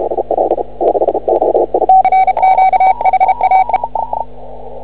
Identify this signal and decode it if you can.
Morse code, heard as a keyed tone, about 32 wpm, 2250 Hz, with noise. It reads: RQ HAE
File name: CW / ha6oi 2